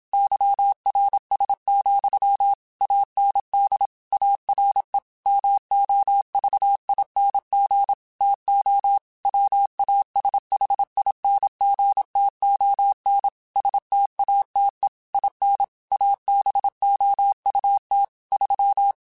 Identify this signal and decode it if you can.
Morse code, heard as a keyed tone, about 27 wpm, 800 Hz. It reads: YRS, AND ARE MOVING TO WASHINGTON STATE IN ABOUT 3
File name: QSOs / QSOs-mp3s / WD0FBT 1@25